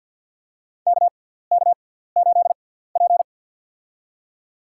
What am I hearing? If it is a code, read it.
Morse code, heard as a keyed tone, about 52 wpm, spaced as for 17 wpm, 700 Hz. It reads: KK8P
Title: USA Call Signs - 4 Characters - 6x Character Spacing - 50wpm